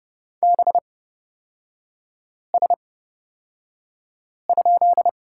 Morse code, heard as a keyed tone, about 30 wpm, spaced as for 5 wpm, 700 Hz. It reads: BS?
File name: Lesson 37e - Letter x - Three Characters 30wpm